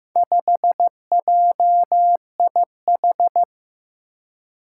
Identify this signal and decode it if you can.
Morse code, heard as a keyed tone, about 15 wpm, 700 Hz. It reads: 5JIH
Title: Four Challenging Letter-Number Combinations 15wpm